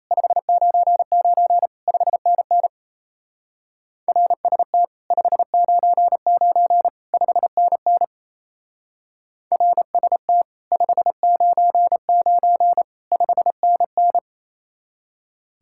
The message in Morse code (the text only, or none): 599 5NN RST 599 5NN RST 599 5NN